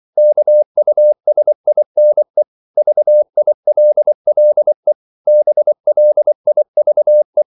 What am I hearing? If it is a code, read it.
Morse code, heard as a keyed tone, about 24 wpm, 600 Hz. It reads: KUSINE VILLE BLIVE
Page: Ferie 24 wpm